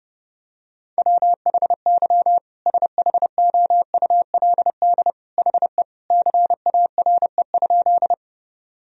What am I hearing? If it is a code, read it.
Morse code, heard as a keyed tone, about 30 wpm, 700 Hz. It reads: WHY SHOULD HE CARE?
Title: Sentences from Top 1000 Words - Review Part 1 - 30wpm